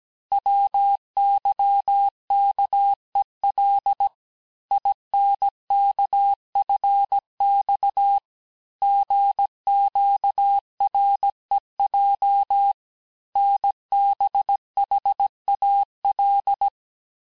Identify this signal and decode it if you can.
Morse code, heard as a keyed tone, about 17 wpm, 800 Hz. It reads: WYKEL INKFX GQREJ NBHAL
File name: cours-du-lundi files / cours-1587-texte-chiffre-vitesse-16MPM